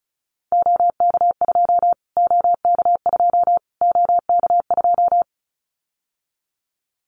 Morse code, heard as a keyed tone, about 34 wpm, 700 Hz. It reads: OK2 OK2 OK2
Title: Callsign Prefixes for Top 25 DXCC - Repeated 3x - 35wpm